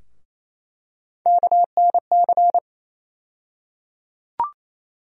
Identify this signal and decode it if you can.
Morse code, heard as a keyed tone, about 28 wpm, 700 Hz. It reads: KNC
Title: USA Call Sign Postfixes - 3 Letters 28wpm